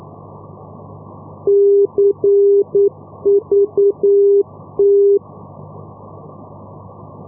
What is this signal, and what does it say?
Morse code, heard as a keyed tone, about 9 wpm, 400 Hz. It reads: CVT
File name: europe / e-448 ndb madrid-cuatro vientos cvt-285 np madrid 2016-05